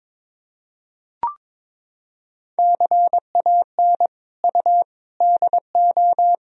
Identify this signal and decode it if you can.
Morse code, heard as a keyed tone, about 22 wpm, 700 Hz. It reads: CAN U DO
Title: Sentences from Top 100 Words and Abbreviations 22wpm